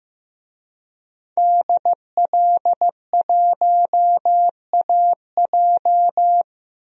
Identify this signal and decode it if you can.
Morse code, heard as a keyed tone, about 15 wpm, 700 Hz. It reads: DL1AJ
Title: DXCC-100 Call Signs - 5 Characters - Repeated 3x - 15wpm